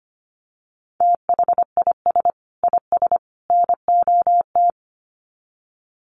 Morse code, heard as a keyed tone, about 25 wpm, 700 Hz. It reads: THIS IS NOT
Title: Sets of 3 Words 25wpm